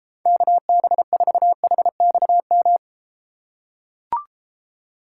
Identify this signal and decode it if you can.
Morse code, heard as a keyed tone, about 31 wpm, 700 Hz. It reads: KB4HXM